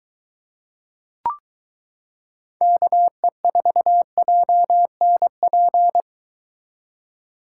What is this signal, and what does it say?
Morse code, heard as a keyed tone, about 22 wpm, 700 Hz. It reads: KE4JNP